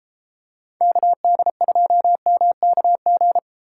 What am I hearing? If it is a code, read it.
Morse code, heard as a keyed tone, about 30 wpm, 700 Hz. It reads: KD2MKG